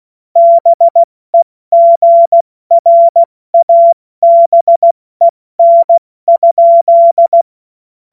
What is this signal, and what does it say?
Morse code, heard as a keyed tone, about 16 wpm, 680 Hz. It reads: BEGRABEN?
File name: Morsekurs / Witze / Witz 23. August 2022 70 F80